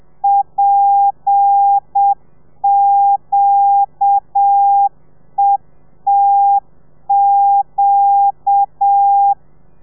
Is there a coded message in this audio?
Yes: Morse code from 0.2 to 9.3 s, heard as a keyed tone, about 7 wpm, 800 Hz. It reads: PQETQ